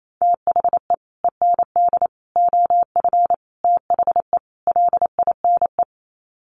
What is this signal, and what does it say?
Morse code, heard as a keyed tone, about 28 wpm, 700 Hz. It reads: THE END OF THE LINE
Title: Sets of 5 Words Encore2 28wpm